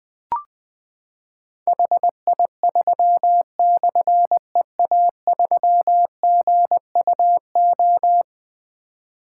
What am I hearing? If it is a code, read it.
Morse code, heard as a keyed tone, about 20 wpm, 700 Hz. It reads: HI3/EA3GUO
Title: Difficult Call Signs 20wpm